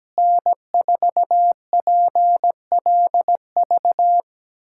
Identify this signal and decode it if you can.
Morse code, heard as a keyed tone, about 17 wpm, 700 Hz. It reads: N4PLV
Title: USA Callsign DE Callsign - 5 Characters - 17wpm